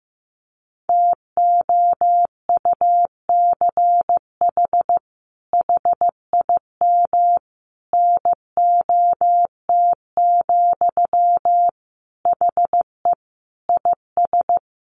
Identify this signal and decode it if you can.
Morse code, heard as a keyed tone, about 15 wpm, 700 Hz. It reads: TOUCH HIM NOT, HE IS
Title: Sentences from Top 1200 Words 15wpm